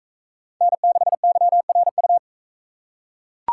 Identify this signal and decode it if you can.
Morse code, heard as a keyed tone, about 43 wpm, 700 Hz. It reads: N6YRU